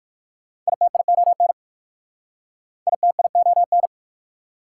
Morse code, heard as a keyed tone, about 59 wpm, 690 Hz. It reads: ITION ITION